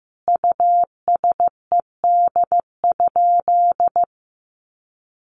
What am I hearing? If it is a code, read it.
Morse code, heard as a keyed tone, about 15 wpm, 700 Hz. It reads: USED?
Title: Sentences from Top 1200 Words 15wpm